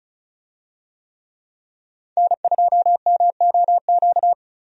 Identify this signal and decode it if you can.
Morse code, heard as a keyed tone, about 34 wpm, 700 Hz. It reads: N2MOQ
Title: USA Call Signs - 5 Characters - 35wpm